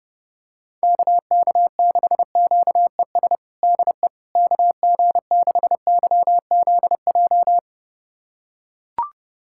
Morse code, heard as a keyed tone, about 30 wpm, 700 Hz. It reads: KK6QES DE KG6YZJ
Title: USA Callsign DE Callsign - 6 Characters - 30wpm